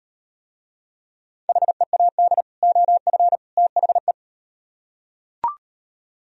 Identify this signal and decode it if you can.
Morse code, heard as a keyed tone, about 36 wpm, 700 Hz. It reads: HEAD OF THE